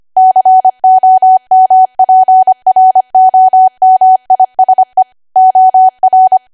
Morse code, heard as a keyed tone, about 25 wpm, 750 Hz. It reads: COMPROMISE OR
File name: Archive / 25 WPM / 25WPM